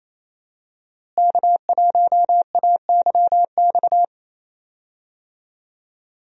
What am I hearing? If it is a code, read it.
Morse code, heard as a keyed tone, about 28 wpm, 700 Hz. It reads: K1AYX
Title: USA Call Signs - 5 Characters - Repeated 3x - 28wpm